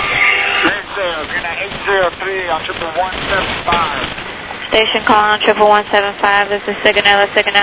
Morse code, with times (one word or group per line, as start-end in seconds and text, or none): none